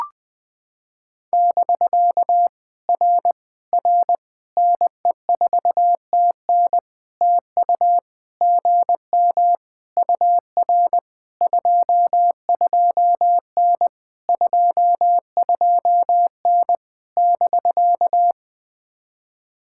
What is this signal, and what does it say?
Morse code, heard as a keyed tone, about 20 wpm, 700 Hz. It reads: <BK> R R NE4TN TU GM UR 22N 22N <BK>
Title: POTA QSO Practice - 20wpm